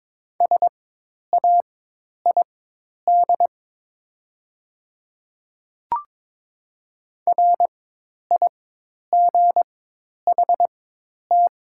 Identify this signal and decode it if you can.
Morse code, heard as a keyed tone, about 22 wpm, spaced as for 10 wpm, 700 Hz. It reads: SAID RIGHT
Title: 4x Character Spacing - Top 200 Words - 22wpm